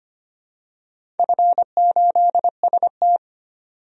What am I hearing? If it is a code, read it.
Morse code, heard as a keyed tone, about 25 wpm, 700 Hz. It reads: F8ST
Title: DXCC-100 Call Signs - 4 Characters - 25wpm